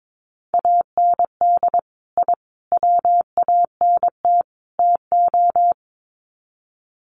Morse code, heard as a keyed tone, about 22 wpm, 700 Hz. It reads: AND I WANT TO
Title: Sets of 4 Words 22wpm